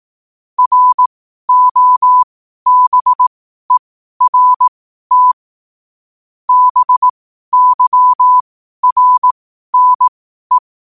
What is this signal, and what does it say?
Morse code, heard as a keyed tone, about 18 wpm, spaced as for 12 wpm, 1000 Hz. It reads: ROBERT BYRNE
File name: qotdfiles / 12wpm